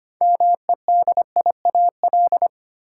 Morse code, heard as a keyed tone, about 25 wpm, 700 Hz. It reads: MEDIAL
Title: Practice QSO - Exchange 3 - More Info - 25wpm